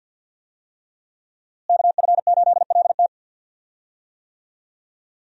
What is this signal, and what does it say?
Morse code, heard as a keyed tone, about 52 wpm, 700 Hz. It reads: KF8LT
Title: USA Call Signs - 5 Characters - Repeated 3x - 50wpm